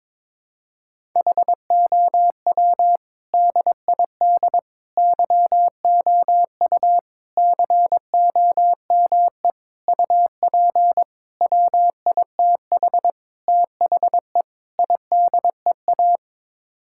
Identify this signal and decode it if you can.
Morse code, heard as a keyed tone, about 22 wpm, 700 Hz. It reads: HOW DID YOU COME UP WITH THE IDEA
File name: practice / 22wpm / Sets of 8 Words - 22wpm